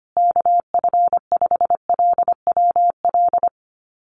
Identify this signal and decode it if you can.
Morse code, heard as a keyed tone, about 25 wpm, 700 Hz. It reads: KF5LWL
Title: USA Call Signs Repeated 2x 25wpm